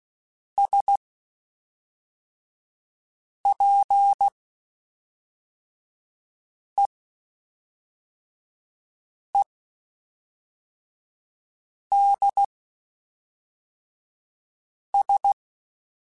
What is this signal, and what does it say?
Morse code, heard as a keyed tone, about 16 wpm, spaced as for 3 wpm, 800 Hz. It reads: SPEEDS
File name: Spaced Characters / Spaced Characters-mp3s / Weather4 14 sp @5